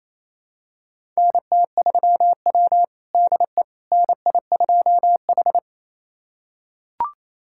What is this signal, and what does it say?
Morse code, heard as a keyed tone, about 28 wpm, 700 Hz. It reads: NT3W DE NI2H
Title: USA Callsign DE Callsign - 4 Characters - 28wpm